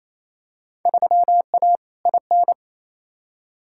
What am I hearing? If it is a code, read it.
Morse code, heard as a keyed tone, about 28 wpm, 700 Hz. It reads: 3A IN
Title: Field Day Exchanges Encore - 28wpm